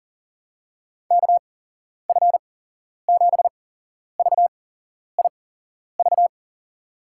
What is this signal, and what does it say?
Morse code, heard as a keyed tone, about 40 wpm, spaced as for 11 wpm, 700 Hz. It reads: KF7VIV
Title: USA Call Signs - 6 Characters - 8x Character Spacing - 40wpm